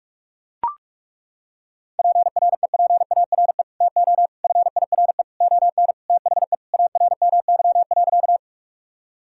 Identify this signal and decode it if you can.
Morse code, heard as a keyed tone, about 41 wpm, 700 Hz. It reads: PREPARE TO FIRE ON THE ARMY.
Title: Sentences from Top 1000 Words - Review Part 6 - 45wpm